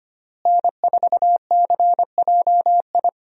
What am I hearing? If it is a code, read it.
Morse code, heard as a keyed tone, about 25 wpm, 700 Hz. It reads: N4CJI
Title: USA Call Signs - 5 Characters - Repeated 2x - 25wpm